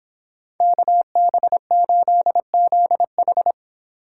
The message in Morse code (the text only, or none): KB8ZH